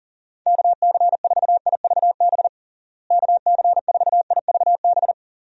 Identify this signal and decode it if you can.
Morse code, heard as a keyed tone, about 40 wpm, 700 Hz. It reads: KC4IVB KC4IVB
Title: USA Call Signs - 6 Characters - Repeated 3x - 40wpm